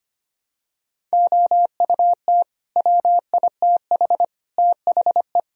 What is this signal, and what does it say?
Morse code, heard as a keyed tone, about 25 wpm, 700 Hz. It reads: OUT WITH THE
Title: Sentences from Top 1000 Words - Review Part 2 - 25wpm